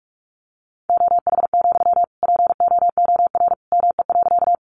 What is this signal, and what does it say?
Morse code, heard as a keyed tone, about 44 wpm, 700 Hz. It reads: OH, POOR ME.
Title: Sentences from Top 1000 Words 45wpm